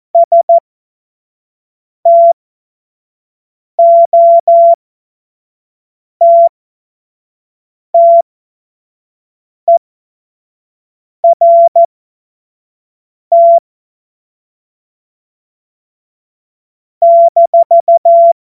Morse code, heard as a keyed tone, about 14 wpm, spaced as for 5 wpm, 680 Hz. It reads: STOTTERT -
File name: Morsekurs / Witze / Witz 31.Oktober 2023 25 F70